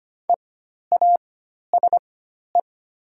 Morse code, heard as a keyed tone, about 25 wpm, spaced as for 12 wpm, 690 Hz. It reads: EASE
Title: 4x Character Spacing - Top 500 Words - 25wpm